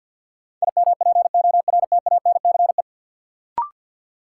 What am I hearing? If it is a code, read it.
Morse code, heard as a keyed tone, about 52 wpm, 700 Hz. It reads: IMPORTANCE
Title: Sets of 3 Words - New Word Drill - 50wpm